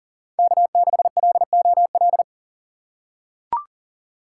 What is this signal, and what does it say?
Morse code, heard as a keyed tone, about 40 wpm, 700 Hz. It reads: K6LOL
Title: USA Call Signs - 5 Characters - 2x Character Spacing - 40wpm